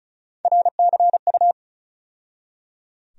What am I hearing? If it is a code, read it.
Morse code, heard as a keyed tone, about 36 wpm, 700 Hz. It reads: RCU
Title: USA Call Sign Postfixes - 3 Letters 35wpm